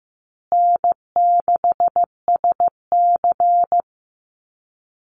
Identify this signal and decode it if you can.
Morse code, heard as a keyed tone, about 15 wpm, 700 Hz. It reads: N6SC
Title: USA Call Signs Repeated 3x 15wpm